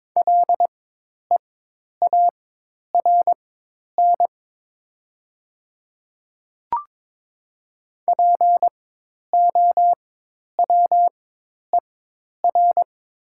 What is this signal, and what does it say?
Morse code, heard as a keyed tone, about 22 wpm, spaced as for 10 wpm, 700 Hz. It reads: LEARN POWER
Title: 4x Character Spacing - Top 500 Words - 22wpm